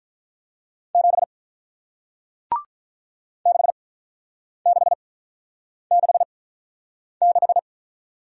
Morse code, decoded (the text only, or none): 7 6666